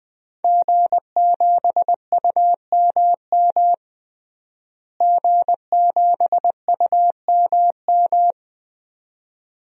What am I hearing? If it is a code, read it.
Morse code, heard as a keyed tone, about 20 wpm, 700 Hz. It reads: G7UMM G7UMM
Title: DXCC-100 Call Signs - 5 Characters - Repeated 2x - 20wpm